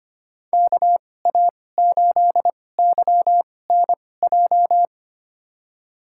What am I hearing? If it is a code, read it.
Morse code, heard as a keyed tone, about 25 wpm, spaced as for 18 wpm, 700 Hz. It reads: KA8YNJ
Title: USA Call Signs - 6 Characters - 2x Character Spacing - 25wpm